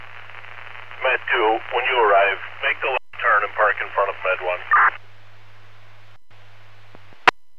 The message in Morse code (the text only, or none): none